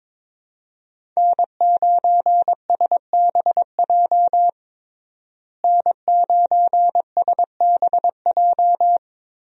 Morse code, heard as a keyed tone, about 22 wpm, 700 Hz. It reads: N9SBJ N9SBJ
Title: USA Call Signs - 5 Characters - Repeated 3x - 22wpm